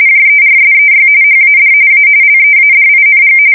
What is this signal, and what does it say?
RTTY, about 45.45 baud. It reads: JERSEY HAD A LEADING
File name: fsk170 / USS-New-Jersey